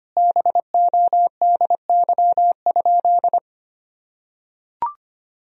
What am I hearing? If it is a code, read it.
Morse code, heard as a keyed tone, about 25 wpm, 700 Hz. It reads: BODY?
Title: Sentences from Top 1000 Words - Review Part 2 - 25wpm